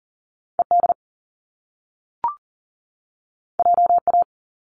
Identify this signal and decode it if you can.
Morse code, heard as a keyed tone, about 42 wpm, 700 Hz. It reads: ED JA
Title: Top Two Letter Combinations in Words 40wpm